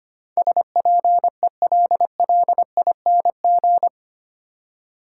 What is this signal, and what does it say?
Morse code, heard as a keyed tone, about 25 wpm, 690 Hz. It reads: SPELLING
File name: 8 Letter Words - First 3 Letters Given - 25wpm